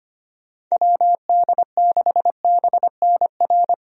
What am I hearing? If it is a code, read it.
Morse code, heard as a keyed tone, about 25 wpm, 700 Hz. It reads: WD6BNR
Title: USA Call Signs - 6 Characters - Repeated 2x - 25wpm